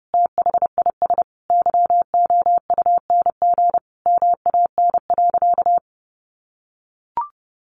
Morse code, heard as a keyed tone, about 30 wpm, 700 Hz. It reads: THIS YOUNG MAN.
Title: Sentences from Top 1300 Words 30wpm